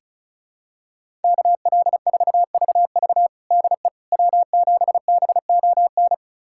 Morse code, heard as a keyed tone, about 36 wpm, 700 Hz. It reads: KL4VV DE W7BON
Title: USA Callsign DE Callsign - 5 Characters - 35wpm